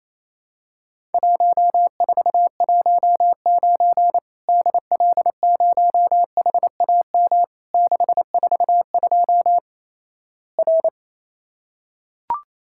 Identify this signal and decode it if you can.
Morse code, heard as a keyed tone, about 28 wpm, 700 Hz. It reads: 1419 DL0HAM 642 R
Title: Worked All Europe DX Contest - QTC Exchange - 28wpm